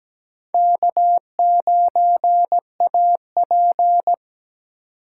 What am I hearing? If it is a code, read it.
Morse code, heard as a keyed tone, about 20 wpm, 700 Hz. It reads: K9AP
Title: Speed Racing - USA Call Signs 17wpm